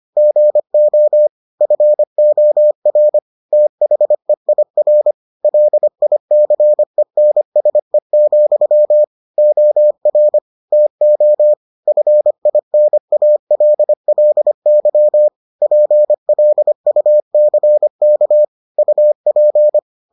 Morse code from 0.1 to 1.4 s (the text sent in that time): GO